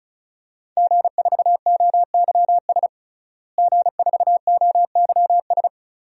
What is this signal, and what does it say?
Morse code, heard as a keyed tone, about 36 wpm, 700 Hz. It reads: G4OYS G4OYS
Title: DXCC-100 Call Signs - 5 Characters - Repeated 2x - 35wpm